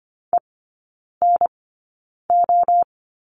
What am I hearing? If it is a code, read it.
Morse code, heard as a keyed tone, about 25 wpm, spaced as for 9 wpm, 690 Hz. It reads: ENO
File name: Lesson 14e - Number 2 - Three Characters 25wpm